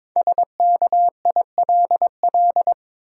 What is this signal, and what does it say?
Morse code, heard as a keyed tone, about 22 wpm, 690 Hz. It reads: SKILL
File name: 5 Letter Words - First 3 Letters Given - 22wpm